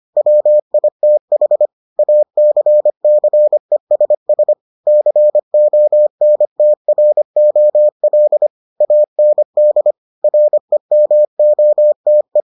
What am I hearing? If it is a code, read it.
Morse code, heard as a keyed tone, about 25 wpm, 600 Hz. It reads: WITH ACCESS CONTROL AND REMOTE